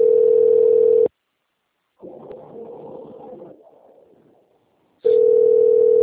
Ringback tone (North America), 440 + 480 Hz.